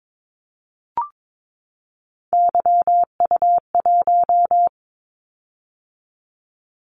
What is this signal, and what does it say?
Morse code, heard as a keyed tone, about 22 wpm, 700 Hz. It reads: YU1
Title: Callsign Prefixes for Top 25 DXCC - 22wpm